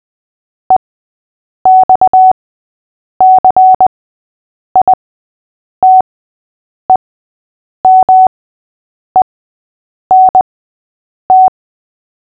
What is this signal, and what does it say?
Morse code, heard as a keyed tone, about 20 wpm, spaced as for 8 wpm, 750 Hz. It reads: EXCITEMENT
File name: cw / fists / Short-Story-6-8wpm